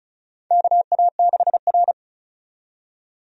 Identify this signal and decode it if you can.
Morse code, heard as a keyed tone, about 36 wpm, 700 Hz. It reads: KA6R
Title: USA Call Signs - 4 Characters - 35wpm